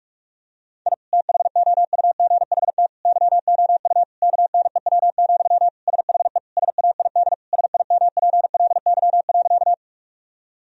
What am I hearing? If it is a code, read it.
Morse code, heard as a keyed tone, about 46 wpm, 700 Hz. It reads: I THOUGHT YOU KNEW, SHE SAID SIMPLY.
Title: Sentences from Top 1000 Words - Review Part 4 - 45wpm